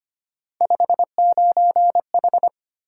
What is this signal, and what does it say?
Morse code, heard as a keyed tone, about 25 wpm, 700 Hz. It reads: 59H
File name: Three Letter-Number Combinations - 25wpm